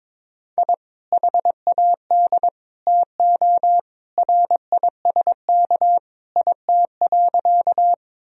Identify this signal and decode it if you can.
Morse code, heard as a keyed tone, about 22 wpm, 700 Hz. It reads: I HAD TO RISK IT.
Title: Sentences from Top 1000 Words - Review Part 4 - 22wpm